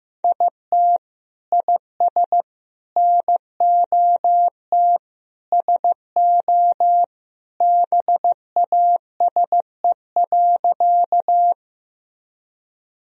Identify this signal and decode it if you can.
Morse code, heard as a keyed tone, about 15 wpm, 700 Hz. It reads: IT IS NOT SO BASE.
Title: Sentences from Top 1000 Words - Review Part 1 - 15wpm